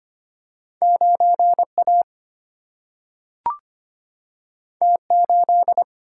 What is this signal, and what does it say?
Morse code, heard as a keyed tone, about 25 wpm, 700 Hz. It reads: 9 A T 8
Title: Single-Letter Single-Number 25wpm